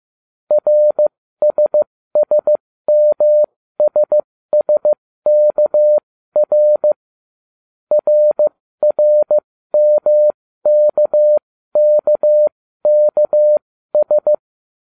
Morse code, heard as a keyed tone, about 15 wpm, spaced as for 13 wpm, 600 Hz. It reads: RSSMSSKR RRMKKKS